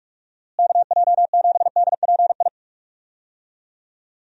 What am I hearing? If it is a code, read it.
Morse code, heard as a keyed tone, about 46 wpm, 700 Hz. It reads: KJ7DPI
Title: USA Call Signs - 6 Characters - Repeated 3x - 45wpm